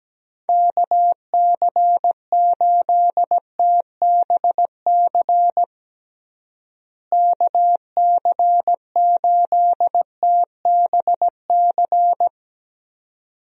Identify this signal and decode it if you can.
Morse code, heard as a keyed tone, about 17 wpm, 700 Hz. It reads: KC8TBC KC8TBC
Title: USA Call Signs - 6 Characters - Repeated 2x - 17wpm